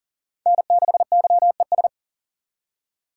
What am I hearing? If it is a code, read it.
Morse code, heard as a keyed tone, about 37 wpm, 690 Hz. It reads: N6YES
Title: USA Call Signs - 5 Characters - 3x Character Spacing - 40wpm